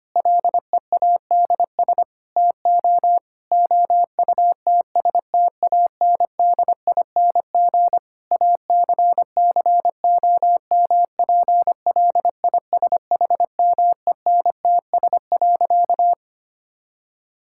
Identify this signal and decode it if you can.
Morse code, heard as a keyed tone, about 25 wpm, 700 Hz. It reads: LEADS TO OUTSTANDING ACCOMPLISHMENTS.
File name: practice / 25wpm / Difficult Sentences with Hints - 25wpm